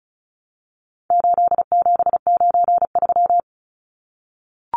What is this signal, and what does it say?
Morse code, heard as a keyed tone, about 34 wpm, 700 Hz. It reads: 8793
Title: Numbers - 4-Digit 35wpm